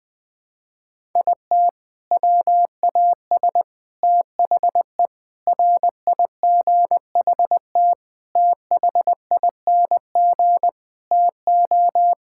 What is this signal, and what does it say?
Morse code, heard as a keyed tone, about 21 wpm, 700 Hz. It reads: IT WAS THE RIGHT THING TO
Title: Sentences with 6-Word Phrases - 20wpm